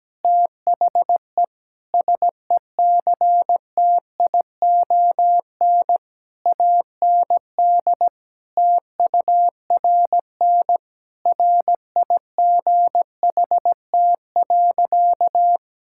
Morse code, heard as a keyed tone, about 17 wpm, 700 Hz. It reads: THE SECTION AND TURN RIGHT.
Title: Sentences from Top 1000 Words - Review Part 5 - 17wpm